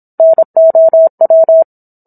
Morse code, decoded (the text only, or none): NOW